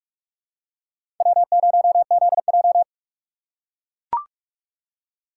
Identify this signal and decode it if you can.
Morse code, heard as a keyed tone, about 43 wpm, 700 Hz. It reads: W0ZJ
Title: USA Call Signs - 2x Character Spacing - 45wpm